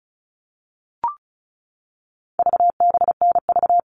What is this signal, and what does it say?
Morse code, heard as a keyed tone, about 34 wpm, 700 Hz. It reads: VBNV
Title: All Cut Numbers - 4-Digit 35wpm